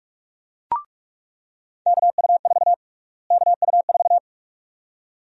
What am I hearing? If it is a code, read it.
Morse code, heard as a keyed tone, about 46 wpm, 700 Hz. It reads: KU4 KU4
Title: USA Call Sign Prefixes - Repeated 2x - 45wpm